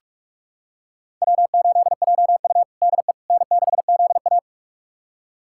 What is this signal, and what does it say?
Morse code, heard as a keyed tone, about 45 wpm, 700 Hz. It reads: W8JU DE N6ZA
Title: USA Callsign DE Callsign - 4 Characters - 45wpm